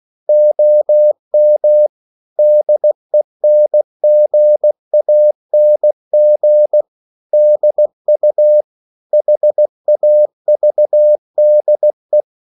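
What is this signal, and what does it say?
Morse code, heard as a keyed tone, about 16 wpm, 600 Hz. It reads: OM DENGANG DU HAVDE
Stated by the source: Slow (QRS) Morse code at 16 wpm, from CW med Gnister.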